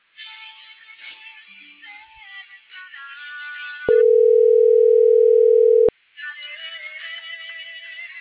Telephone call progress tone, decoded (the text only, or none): ringback tone